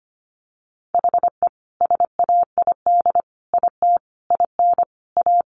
Morse code, heard as a keyed tone, about 25 wpm, 700 Hz. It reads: HE SAID IT IN A
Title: Sentences from Top 1500 Words 25wpm